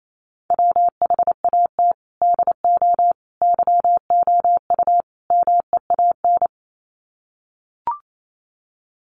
Morse code, heard as a keyed tone, about 28 wpm, 700 Hz. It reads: WHAT DO YOU MEAN
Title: Sets of 4 Words 28wpm